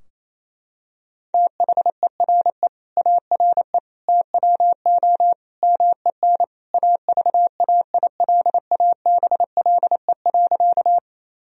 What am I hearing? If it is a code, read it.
Morse code, heard as a keyed tone, about 28 wpm, 700 Hz. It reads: THERE ARE TWO MEN AVAILABLE.
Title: Sentences from Top 1000 Words - Review Part 3 - 28wpm